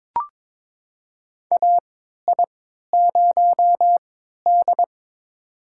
Morse code, heard as a keyed tone, about 22 wpm, spaced as for 12 wpm, 700 Hz. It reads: AI0D
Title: USA Call Signs - 4 Characters - 3x Character Spacing - 22wpm